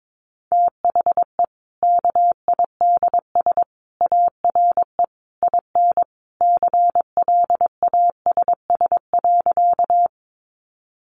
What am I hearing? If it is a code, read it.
Morse code, heard as a keyed tone, about 22 wpm, 700 Hz. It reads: THE KIDS ARE IN CLASS.
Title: Easy Phrases 22wpm